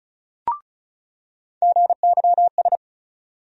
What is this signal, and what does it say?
Morse code, heard as a keyed tone, about 34 wpm, 700 Hz. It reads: GYS
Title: USA Call Sign Postfixes - 3 Letters 35wpm